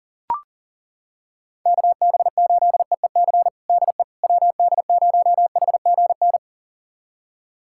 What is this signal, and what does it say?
Morse code, heard as a keyed tone, about 40 wpm, 700 Hz. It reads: KB8EEC DE WD0HGN
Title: USA Callsign DE Callsign - 6 Characters - 40wpm